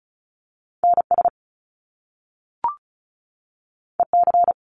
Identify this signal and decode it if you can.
Morse code, heard as a keyed tone, about 36 wpm, 700 Hz. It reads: NS EC
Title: Top Two Letter Combinations in Words 35wpm